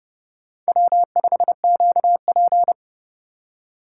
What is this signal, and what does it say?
Morse code, heard as a keyed tone, about 30 wpm, 700 Hz. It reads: W5QP
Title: K1USN SST QSO Practice - 30wpm